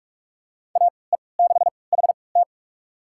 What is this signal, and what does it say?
Morse code, heard as a keyed tone, about 46 wpm, spaced as for 25 wpm, 700 Hz. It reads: AE6HT
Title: USA Call Signs - 5 Characters - 3x Character Spacing - 45wpm